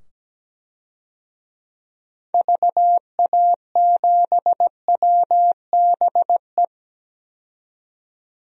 Morse code, heard as a keyed tone, about 17 wpm, 700 Hz. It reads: VA7WBE
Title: DXCC-100 Call Signs - 6 Characters - 17wpm